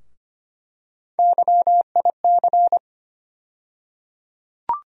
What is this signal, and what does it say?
Morse code, heard as a keyed tone, about 25 wpm, 700 Hz. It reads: YIC
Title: USA Call Sign Postfixes - 3 Letters 25wpm